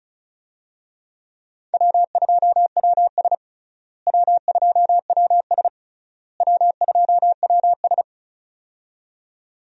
Morse code, heard as a keyed tone, about 36 wpm, 700 Hz. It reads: W2WS W2WS W2WS
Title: USA Call Signs - 4 Characters - Repeated 3x - 35wpm